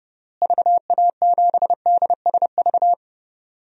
Morse code, heard as a keyed tone, about 30 wpm, 700 Hz. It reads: VA7DSV
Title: DXCC-100 Call Signs - 6 Characters - Repeated 3x - 30wpm